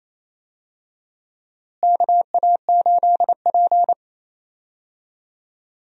Morse code, heard as a keyed tone, about 28 wpm, 700 Hz. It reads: KA8P
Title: USA Call Signs - 4 Characters - Repeated 2x - 28wpm